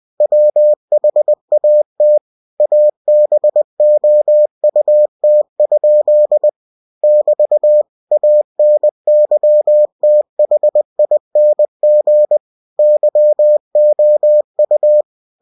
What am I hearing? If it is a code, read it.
Morse code, heard as a keyed tone, about 20 wpm, 600 Hz. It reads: WHAT ABOUT? <BT> ANYTHING YOU
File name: cw-books / winnie-the-pooh / winnie-20wpm-page-0065